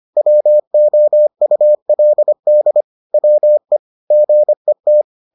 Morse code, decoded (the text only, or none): WOULD WE GET